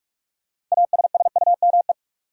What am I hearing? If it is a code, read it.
Morse code, heard as a keyed tone, about 45 wpm, 700 Hz. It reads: ASSUME
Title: Sentences from Top 1500 Words - New Word Drill - 45wpm